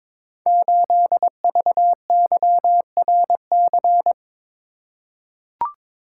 Morse code, heard as a keyed tone, about 22 wpm, 700 Hz. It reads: 8VYRC
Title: Code Groups - 5 Letter:Numbers - 22wpm